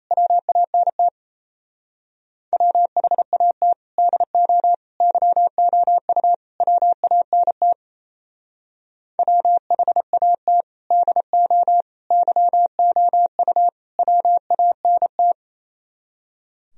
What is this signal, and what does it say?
Morse code, heard as a keyed tone, about 33 wpm, 700 Hz. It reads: WANT WHAT DO YOU WANT WHAT DO YOU WANT
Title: Speed-Racing - Sets of 4 Words 28wpm